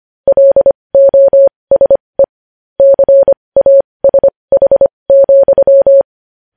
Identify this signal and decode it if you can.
Morse code, heard as a keyed tone, about 25 wpm, spaced as for 20 wpm, 550 Hz. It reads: LOSE CASH,
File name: podcast / News headlines-1x-25x20